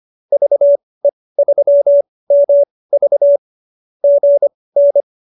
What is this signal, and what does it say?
Morse code, heard as a keyed tone, about 25 wpm, spaced as for 18 wpm, 570 Hz. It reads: VE3MV GN